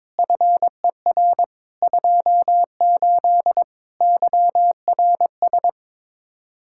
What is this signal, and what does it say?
Morse code, heard as a keyed tone, about 22 wpm, 700 Hz. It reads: FER 28 YRS
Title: Practice QSO - Exchange 3 - More Info - 22wpm